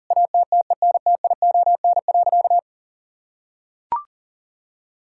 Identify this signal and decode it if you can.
Morse code, heard as a keyed tone, about 40 wpm, 700 Hz. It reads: ATTENTION.
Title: Sentences from Top 1000 Words - Review Part 6 - 40wpm